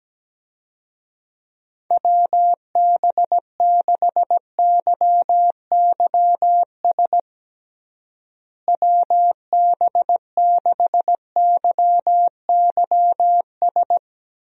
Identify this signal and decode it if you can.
Morse code, heard as a keyed tone, about 17 wpm, 700 Hz. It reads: WB6YYS WB6YYS
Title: USA Call Signs - 6 Characters - Repeated 2x - 17wpm